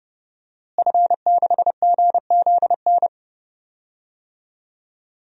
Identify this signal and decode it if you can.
Morse code, heard as a keyed tone, about 31 wpm, 700 Hz. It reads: F6GZN
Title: DXCC-100 Call Signs - 5 Characters - Repeated 3x - 30wpm